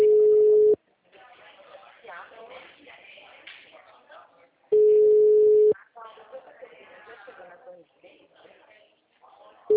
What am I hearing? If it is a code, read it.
Ringback tone (Europe, 425 Hz), with speech.